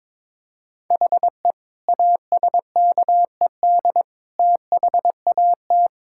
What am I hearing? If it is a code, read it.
Morse code, heard as a keyed tone, about 22 wpm, 700 Hz. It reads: HE ASKED THAT
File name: Sentences with 9-Word Phrases - 22wpm